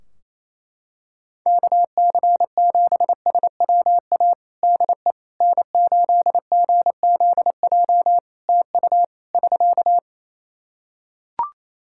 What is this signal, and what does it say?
Morse code, heard as a keyed tone, about 28 wpm, 700 Hz. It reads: KC7SWA DE N8GZJ TU <SK>
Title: Practice QSO - Exchange 4 - Ending - 28wpm